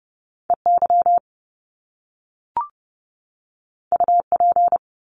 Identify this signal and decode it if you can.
Morse code, heard as a keyed tone, about 30 wpm, 700 Hz. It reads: EY UP
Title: Top Two Letter Combinations in Words 30wpm